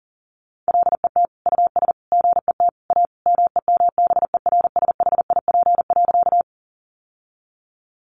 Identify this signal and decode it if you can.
Morse code, heard as a keyed tone, about 40 wpm, 700 Hz. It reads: LET US GET A MEMBERSHIP.
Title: Sentences from Top 1400 Words 40wpm